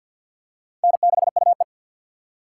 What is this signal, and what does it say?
Morse code, heard as a keyed tone, about 48 wpm, 690 Hz. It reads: N6UE
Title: USA Call Signs - 4 Characters - 6x Character Spacing - 50wpm